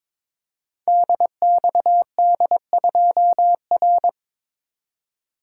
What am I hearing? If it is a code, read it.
Morse code, heard as a keyed tone, about 22 wpm, 700 Hz. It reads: DXD2R
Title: Code Groups - 5 Letter:Numbers - 22wpm